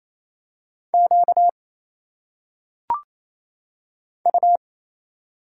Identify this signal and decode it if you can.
Morse code, heard as a keyed tone, about 27 wpm, 700 Hz. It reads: Q U
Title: Single Letters - Phonetic Pronunciation 28wpm